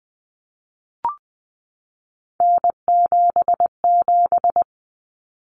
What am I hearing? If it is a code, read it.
Morse code, heard as a keyed tone, about 20 wpm, 700 Hz. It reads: N77
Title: Common Cut Numbers - 3-Digit 20wpm